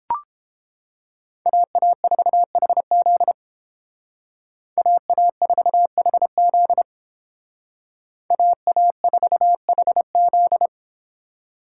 Morse code, heard as a keyed tone, about 30 wpm, 700 Hz. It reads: AA4HZ AA4HZ AA4HZ